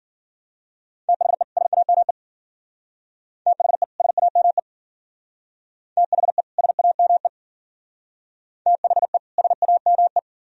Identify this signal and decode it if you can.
Morse code, heard as a keyed tone, about 50 wpm, 700 Hz. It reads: THE SAME THE SAME THE SAME THE SAME
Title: Speed-Racing - Sets of 2 Words 40wpm